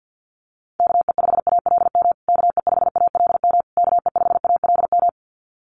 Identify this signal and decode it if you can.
Morse code, heard as a keyed tone, about 50 wpm, 700 Hz. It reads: KE5ALM KE5ALM KE5ALM
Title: USA Call Signs Repeated 3x 50wpm